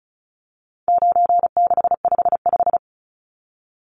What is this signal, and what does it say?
Morse code, heard as a keyed tone, about 34 wpm, 700 Hz. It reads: 9655
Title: Numbers - 4-Digit 35wpm